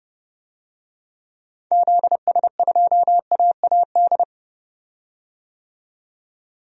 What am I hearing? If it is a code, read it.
Morse code, heard as a keyed tone, about 30 wpm, 690 Hz. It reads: ZS2AAD
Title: DXCC-100 Call Signs - 6 Characters - 30wpm